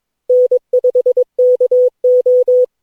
Morse code, heard as a keyed tone, about 22 wpm, 500 Hz. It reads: N5KO
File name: cw / Session9 cw training